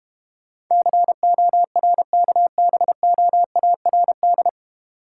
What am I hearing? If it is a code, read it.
Morse code, heard as a keyed tone, about 31 wpm, 700 Hz. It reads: CORKBOARD
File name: practice / 25wpm / Difficult Words - Speed-Racing with Hints - 25wpm